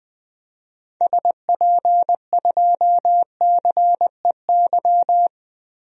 Morse code, heard as a keyed tone, about 21 wpm, 700 Hz. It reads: SP2CEY
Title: DXCC-100 Call Signs - 6 Characters - Repeated 3x - 20wpm